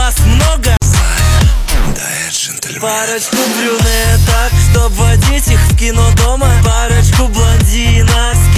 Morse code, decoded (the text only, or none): none